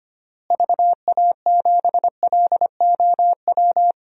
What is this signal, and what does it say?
Morse code, heard as a keyed tone, about 25 wpm, 700 Hz. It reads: VA7LOW
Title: DXCC-100 Call Signs - 6 Characters - Repeated 3x - 25wpm